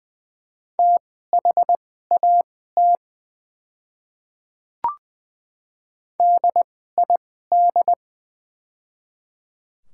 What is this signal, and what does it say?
Morse code, heard as a keyed tone, about 20 wpm, spaced as for 14 wpm, 700 Hz. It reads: THAT DID
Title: 2x Character Spacing - Top 200 Words - 20wpm